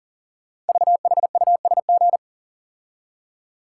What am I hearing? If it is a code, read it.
Morse code, heard as a keyed tone, about 39 wpm, 690 Hz. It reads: VHUSG
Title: Code Groups - 5 Letters - 40wpm